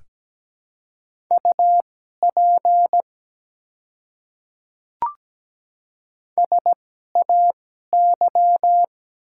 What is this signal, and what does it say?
Morse code, heard as a keyed tone, about 17 wpm, spaced as for 12 wpm, 700 Hz. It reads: UP SAY
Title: 2x Character Spacing - Top 300 Words - 17wpm